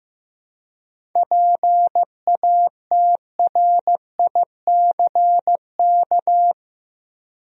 Morse code, heard as a keyed tone, about 15 wpm, 700 Hz. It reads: PATRICK
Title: USA Common Names - 15wpm